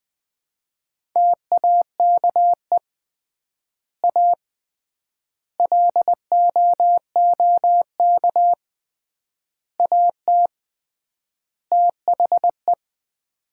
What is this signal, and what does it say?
Morse code, heard as a keyed tone, about 21 wpm, 700 Hz. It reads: TAKE A LOOK AT THE
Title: 3x Word-Spacing - Sets of 5 Words - 20wpm